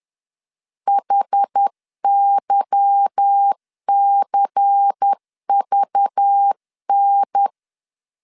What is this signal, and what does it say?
Morse code, heard as a keyed tone, about 10 wpm, 800 Hz. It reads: HYCVN
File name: LEZIONI / 9 28 10wpm 2842